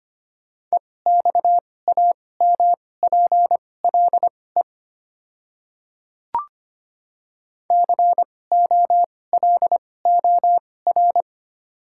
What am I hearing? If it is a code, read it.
Morse code, heard as a keyed tone, about 25 wpm, spaced as for 18 wpm, 700 Hz. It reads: EXAMPLE COLOR
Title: 2x Character Spacing - Top 300 Words - 25wpm